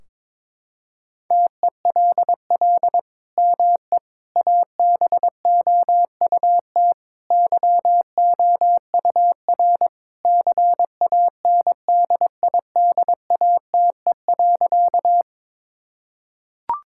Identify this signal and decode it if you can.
Morse code, heard as a keyed tone, about 22 wpm, 700 Hz. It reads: TELL ME ABOUT YOUR CANDIDATE.
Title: Sentences from Top 1600 Words 22wpm